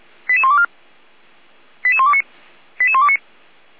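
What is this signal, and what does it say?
Five-tone selective call (ZVEI), heard as quick runs of five tones, tones of about 70 ms. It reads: 88124 88129 88129
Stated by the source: ZVEI five-tone call received by SDR.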